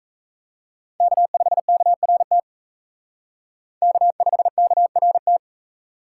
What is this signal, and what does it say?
Morse code, heard as a keyed tone, about 36 wpm, 700 Hz. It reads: K5KRT K5KRT